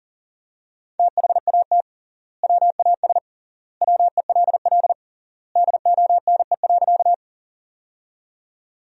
Morse code, heard as a keyed tone, about 37 wpm, 700 Hz. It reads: THAT WAS WELL DONE.
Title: 3x Word-Spacing - Sentences from Top 500 Words - 40wpm